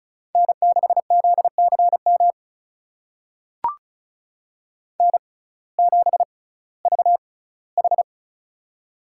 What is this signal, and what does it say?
Morse code, heard as a keyed tone, about 36 wpm, spaced as for 19 wpm, 700 Hz. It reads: N6ZCM N7VH
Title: USA Call Signs - 6x Character Spacing - 35wpm